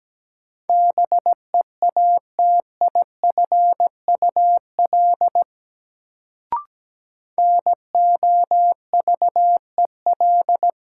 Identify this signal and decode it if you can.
Morse code, heard as a keyed tone, about 17 wpm, 700 Hz. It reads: BEATIFUL NOVEL
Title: Sentences from Top 1300 Words - New Word Drill - 17wpm